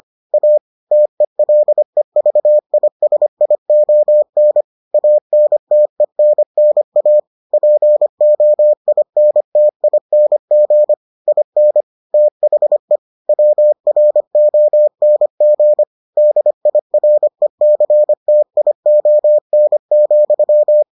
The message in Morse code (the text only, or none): A TELEVISION ANTENNA POINTING IN THE WRONG DIRECTION,